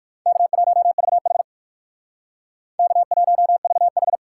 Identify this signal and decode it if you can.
Morse code, heard as a keyed tone, about 59 wpm, 700 Hz. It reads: K1VH K1VH